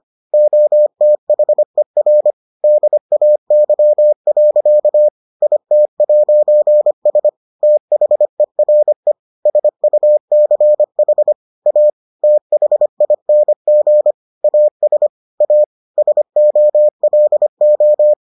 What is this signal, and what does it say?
Morse code, heard as a keyed tone, about 25 wpm, 600 Hz. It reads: OTHER DAY. IT'S THERE SUCH A THING AS A SOLO